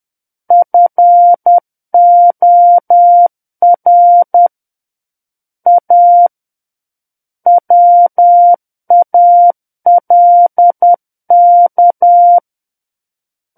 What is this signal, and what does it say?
Morse code, heard as a keyed tone, about 10 wpm, 700 Hz. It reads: FOR A WALK
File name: Book One / 01-Chapter 1 - 10 WPM